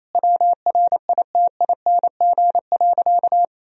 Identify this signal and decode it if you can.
Morse code, heard as a keyed tone, about 28 wpm, 700 Hz. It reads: WRITING.
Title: Sentences from Top 1000 Words - Review Part 1 - 28wpm